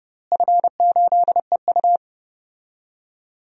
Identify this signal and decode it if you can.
Morse code, heard as a keyed tone, about 30 wpm, 700 Hz. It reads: F8EU
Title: DXCC-100 Call Signs - 4 Characters - Repeated 2x - 30wpm